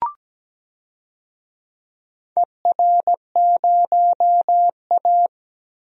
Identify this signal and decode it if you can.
Morse code, heard as a keyed tone, about 17 wpm, 700 Hz. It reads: ER0A
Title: DXCC-100 Call Signs - 4 Characters - 17wpm